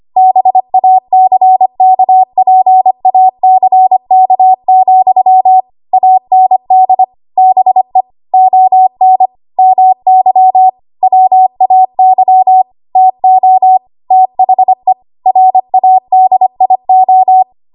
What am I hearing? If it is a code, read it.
Morse code, heard as a keyed tone, about 25 wpm, 750 Hz. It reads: BACKPACK, AND BE ON MY WAY TO THE RADIO
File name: Morse / 25WPM